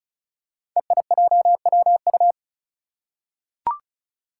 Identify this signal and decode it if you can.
Morse code, heard as a keyed tone, about 36 wpm, 700 Hz. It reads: EIJWU
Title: Five Challenging Letter-Number Combinations 35wpm